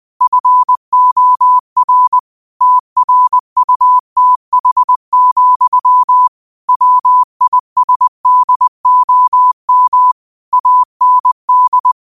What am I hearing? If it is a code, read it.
Morse code, heard as a keyed tone, about 20 wpm, 1000 Hz. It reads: FOR TRUTH, WISDOM AND